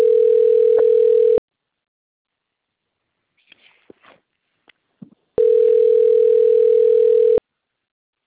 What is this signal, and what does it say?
Ringback tone (North America), 440 + 480 Hz.